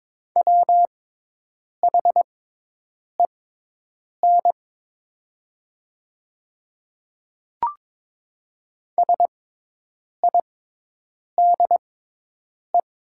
Morse code, heard as a keyed tone, about 22 wpm, spaced as for 8 wpm, 700 Hz. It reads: WHEN SIDE
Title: 6x Character Spacing - Top 100 Words - 22wpm